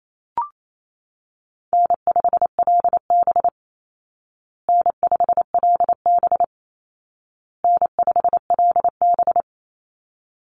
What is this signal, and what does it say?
Morse code, heard as a keyed tone, about 28 wpm, 700 Hz. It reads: N5LB N5LB N5LB
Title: Easy Call Signs Repeated 3x - 28wpm